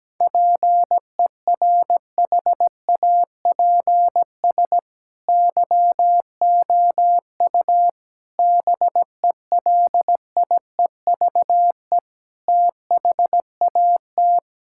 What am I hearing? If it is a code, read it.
Morse code, heard as a keyed tone, about 17 wpm, 700 Hz. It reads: PERHAPS YOU BELIEVE THAT
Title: Sentences from Top 1600 Words 17wpm